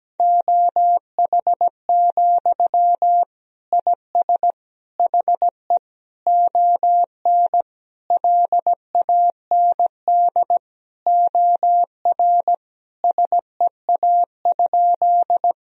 Morse code, heard as a keyed tone, about 17 wpm, 700 Hz. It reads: OH, IS HE ON LAND OR SEA?
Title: Sentences from Top 1000 Words - Review Part 1 - 17wpm